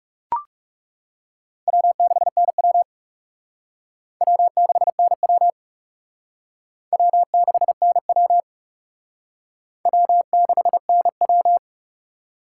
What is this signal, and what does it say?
Morse code, heard as a keyed tone, about 35 wpm, 700 Hz. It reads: W6NW W6NW W6NW W6NW
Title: Speed Racing - USA 1x2 Call Signs 30wpm